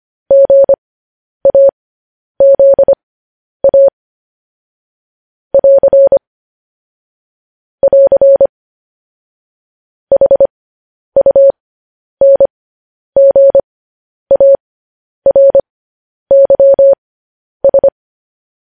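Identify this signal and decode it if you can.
Morse code, heard as a keyed tone, about 25 wpm, spaced as for 10 wpm, 550 Hz. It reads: GAZA <AR> <AR> HUNGARYS
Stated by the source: Morse code practice sending news headlines at 10 wpm.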